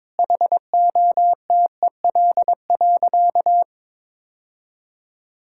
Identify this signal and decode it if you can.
Morse code, heard as a keyed tone, about 21 wpm, 700 Hz. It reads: HOTEL.
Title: Sentences from Top 1600 Words 22wpm